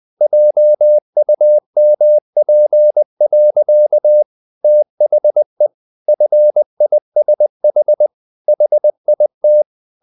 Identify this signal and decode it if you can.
Morse code, heard as a keyed tone, about 20 wpm, 600 Hz. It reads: JUMP. THE FISH HIT
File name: the old man and the sea part 2 20wpm